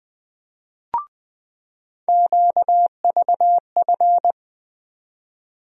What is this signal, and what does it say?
Morse code, heard as a keyed tone, about 21 wpm, 700 Hz. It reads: QVF
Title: USA Call Sign Postfixes - 3 Letters 20wpm